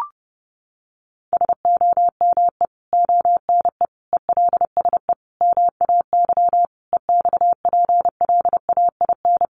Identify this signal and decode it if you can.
Morse code, heard as a keyed tone, about 30 wpm, 700 Hz. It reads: SOME ONE ELSE MAY EXPLAIN
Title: Sentences from Top 1100 Words 30wpm